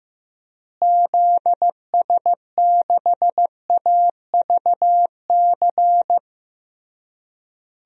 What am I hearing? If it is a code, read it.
Morse code, heard as a keyed tone, about 16 wpm, 700 Hz. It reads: ZS6AVC
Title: DXCC-100 Call Signs - 6 Characters - Repeated 3x - 15wpm